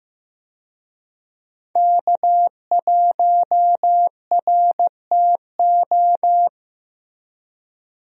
Morse code, heard as a keyed tone, about 15 wpm, 700 Hz. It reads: K1RTO
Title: USA Call Signs - 5 Characters - 15wpm